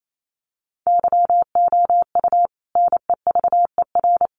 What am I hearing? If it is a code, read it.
Morse code, heard as a keyed tone, about 28 wpm, 700 Hz. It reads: YOU NEVER
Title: Sentences from Top 1500 Words 28wpm